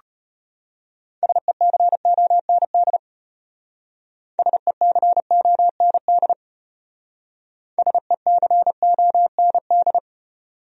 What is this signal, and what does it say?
Morse code, heard as a keyed tone, about 33 wpm, 700 Hz. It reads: SECOND SECOND SECOND